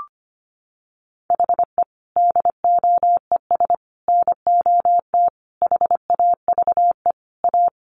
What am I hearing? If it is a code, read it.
Morse code, heard as a keyed tone, about 25 wpm, 700 Hz. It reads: HE DOES NOT HAVE A
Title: Sets of 5 Words Encore 25wpm